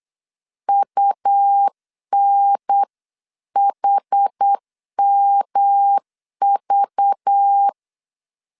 Morse code, heard as a keyed tone, about 8 wpm, 800 Hz. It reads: UNHMV
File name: 8wpm